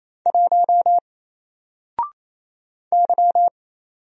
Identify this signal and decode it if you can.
Morse code, heard as a keyed tone, about 28 wpm, 700 Hz. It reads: 1 Y
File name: Single-Letter-Number-28wpm